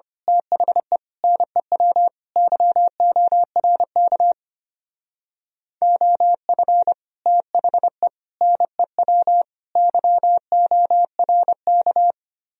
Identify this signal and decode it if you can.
Morse code, heard as a keyed tone, about 27 wpm, 700 Hz. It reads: THE NEW YORK OF THE NEW YORK
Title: Speed-Racing - Sets of 4 Words 25wpm